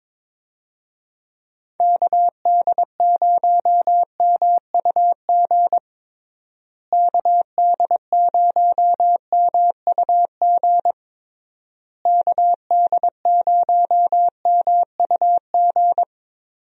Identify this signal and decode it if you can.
Morse code, heard as a keyed tone, about 22 wpm, 700 Hz. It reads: KD0MUG KD0MUG KD0MUG
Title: USA Call Signs - 6 Characters - Repeated 3x - 22wpm